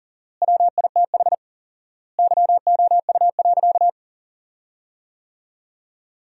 Morse code, heard as a keyed tone, about 40 wpm, 700 Hz. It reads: WITH YOU.
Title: 4x Word-Spacing - Sentences from Top 500 Words - 40wpm